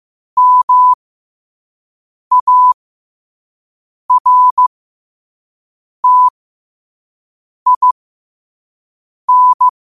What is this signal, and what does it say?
Morse code, heard as a keyed tone, about 15 wpm, spaced as for 5 wpm, 1000 Hz. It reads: MARTIN
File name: qotdfiles / 5wpm